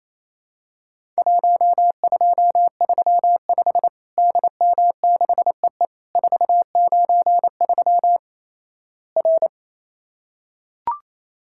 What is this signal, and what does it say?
Morse code, heard as a keyed tone, about 28 wpm, 700 Hz. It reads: 1235 DM6EE 493 R
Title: Worked All Europe DX Contest - QTC Exchange - 28wpm